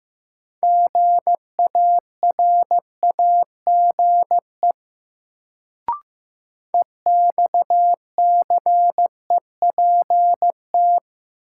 Morse code, heard as a keyed tone, about 15 wpm, 700 Hz. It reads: GARAGE EXCEPT
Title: Sentences from Top 1600 Words - New Word Drill Alt - 15wpm